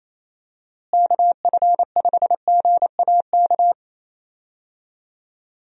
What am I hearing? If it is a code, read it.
Morse code, heard as a keyed tone, about 28 wpm, 700 Hz. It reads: KF5GAK
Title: USA Call Signs - 6 Characters - Repeated 3x - 28wpm